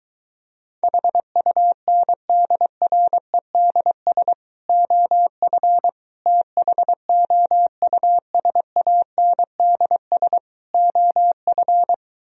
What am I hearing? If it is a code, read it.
Morse code, heard as a keyed tone, about 23 wpm, 700 Hz. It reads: HUNDREDS OF THOUSANDS OF